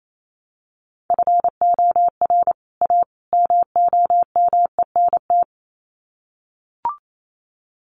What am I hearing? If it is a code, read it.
Morse code, heard as a keyed tone, about 28 wpm, 700 Hz. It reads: FOR A MOMENT
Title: Sets of 3 Words 28wpm